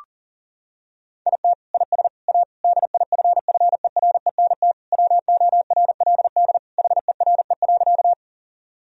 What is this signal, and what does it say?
Morse code, heard as a keyed tone, about 40 wpm, 700 Hz. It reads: IT IS A DIFFERENT WORLD HERE.
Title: Sentences from Top 1000 Words - Review Part 3 - 40wpm